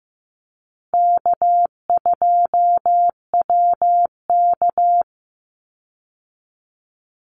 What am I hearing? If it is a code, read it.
Morse code, heard as a keyed tone, about 14 wpm, 700 Hz. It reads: K2WK
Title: Easy Call Signs Repeated 2x - 15wpm